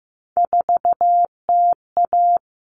Morse code, heard as a keyed tone, about 15 wpm, 700 Hz. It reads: 4TA
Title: All Cut Numbers - 3-Digit 15wpm